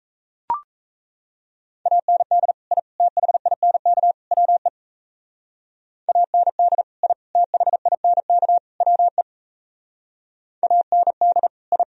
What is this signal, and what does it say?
Morse code, heard as a keyed tone, about 34 wpm, 700 Hz. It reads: AND I THINK WE AND I THINK WE AND I
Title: Speed-Racing - Sets of 4 Words 28wpm